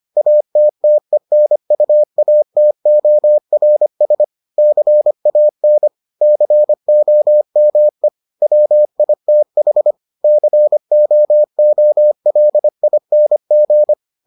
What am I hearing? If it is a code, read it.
Morse code, heard as a keyed tone, about 25 wpm, 600 Hz. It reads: ATTENUATORS CAN COME WITH COOLING